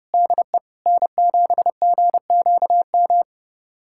Morse code, heard as a keyed tone, about 30 wpm, 700 Hz. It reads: DE N7GQM
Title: USA Callsign DE Callsign 30wpm